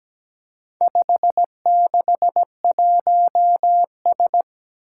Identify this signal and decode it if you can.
Morse code, heard as a keyed tone, about 17 wpm, 700 Hz. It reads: 561S
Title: Four Challenging Letter-Number Combinations 17wpm